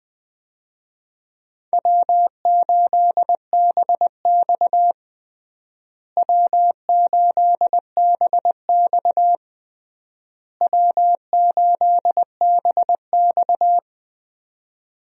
Morse code, heard as a keyed tone, about 20 wpm, 700 Hz. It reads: W8BX W8BX W8BX
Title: USA Call Signs - 4 Characters - Repeated 3x - 20wpm